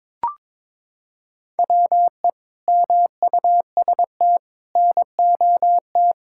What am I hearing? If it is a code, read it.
Morse code, heard as a keyed tone, about 22 wpm, 700 Hz. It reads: WE MUST NOT
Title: Sentences from Top 1600 Words 22wpm